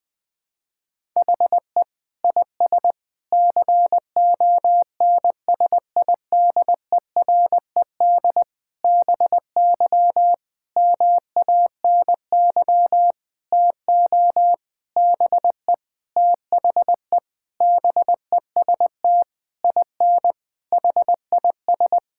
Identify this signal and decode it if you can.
Morse code, heard as a keyed tone, about 20 wpm, 700 Hz. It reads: HE IS CONSIDERED BY MANY TO BE THE BEST IN HIS
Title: Sentences with 7-Word Phrases - 20wpm